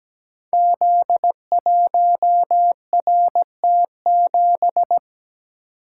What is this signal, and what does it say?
Morse code, heard as a keyed tone, about 17 wpm, 700 Hz. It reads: Z1RT7
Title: Code Groups - 5 Letter:Numbers - 17wpm